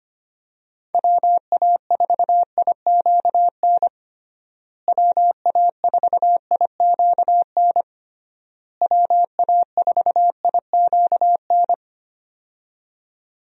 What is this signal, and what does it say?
Morse code, heard as a keyed tone, about 25 wpm, 700 Hz. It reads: WA4IQN WA4IQN WA4IQN
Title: USA Call Signs - 6 Characters - Repeated 3x - 25wpm